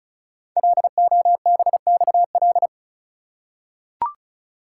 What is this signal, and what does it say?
Morse code, heard as a keyed tone, about 36 wpm, 700 Hz. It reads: LOBXL
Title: Code Groups - 5 Letters - 35wpm